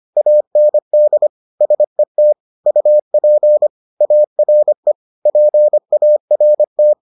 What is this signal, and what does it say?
Morse code, heard as a keyed tone, about 25 wpm, 600 Hz. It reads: AND SET UP ARE PART